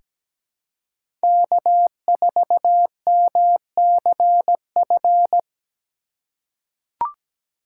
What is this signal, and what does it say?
Morse code, heard as a keyed tone, about 17 wpm, 700 Hz. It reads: K4MCF
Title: USA Call Signs - 5 Characters - 4x Character Spacing - 17wpm